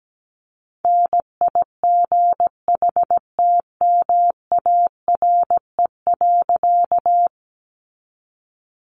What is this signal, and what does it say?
Morse code, heard as a keyed tone, about 17 wpm, 700 Hz. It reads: NIGHTMARE.
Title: A Sound Of Thunder 17wpm